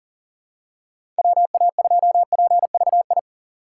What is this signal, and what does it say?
Morse code, heard as a keyed tone, about 40 wpm, 700 Hz. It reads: WA2PVI
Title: USA Call Signs - 6 Characters - 6x Character Spacing - 40wpm